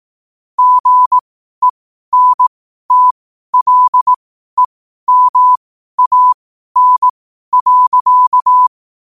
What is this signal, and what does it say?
Morse code, heard as a keyed tone, about 18 wpm, spaced as for 12 wpm, 1000 Hz. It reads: GENTLEMAN.
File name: qotdfiles / 12wpm